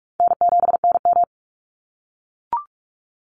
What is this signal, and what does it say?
Morse code, heard as a keyed tone, about 44 wpm, 700 Hz. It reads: N7NM
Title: Call Signs - 45wpm